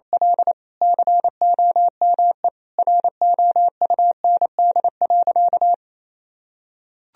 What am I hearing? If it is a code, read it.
Morse code, heard as a keyed tone, about 28 wpm, 700 Hz. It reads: L COME ROUND.
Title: Sentences from Top 1000 Words - Review Part 1 - 28wpm